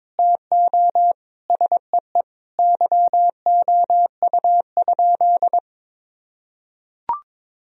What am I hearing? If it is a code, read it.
Morse code, heard as a keyed tone, about 22 wpm, 700 Hz. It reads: TO SEE YOU?
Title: Sentences from Top 1000 Words - Review Part 2 - 22wpm